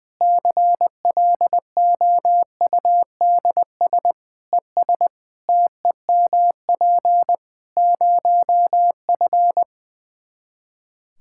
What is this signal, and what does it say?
Morse code, heard as a keyed tone, about 20 wpm, 700 Hz. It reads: CLOUDS ES TEMP 0F
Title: Practice QSO - Exchange 2 - Info - 20wpm